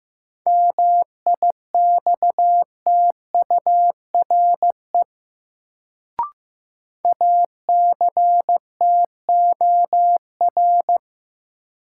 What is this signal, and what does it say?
Morse code, heard as a keyed tone, about 15 wpm, 700 Hz. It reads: MIXTURE ACTOR
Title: Sentences from Top 1500 Words - New Word Drill - 15wpm